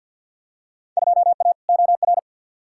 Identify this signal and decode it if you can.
Morse code, heard as a keyed tone, about 51 wpm, 700 Hz. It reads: 2A OR
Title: Field Day Exchanges - 50wpm